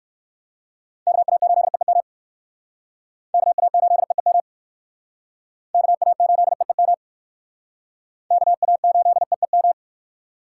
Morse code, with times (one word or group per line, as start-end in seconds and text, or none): 1.1-2.0 KA8EEM
3.3-4.4 KA8EEM
5.7-6.9 KA8EEM
8.3-9.7 KA8EEM